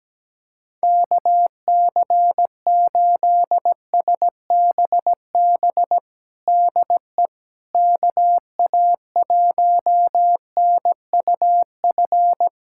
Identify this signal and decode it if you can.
Morse code, heard as a keyed tone, about 17 wpm, 700 Hz. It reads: KC8SBB DE KA1NUF
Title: USA Callsign DE Callsign - 6 Characters - 17wpm